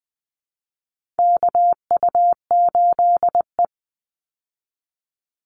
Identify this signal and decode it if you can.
Morse code, heard as a keyed tone, about 20 wpm, 700 Hz. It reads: KU8E
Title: Call Signs Repeated 3x - 20wpm